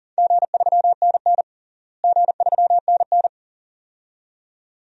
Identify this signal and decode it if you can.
Morse code, heard as a keyed tone, about 40 wpm, 700 Hz. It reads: G3NN G3NN
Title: DXCC-100 Call Signs - 4 Characters - Repeated 2x - 40wpm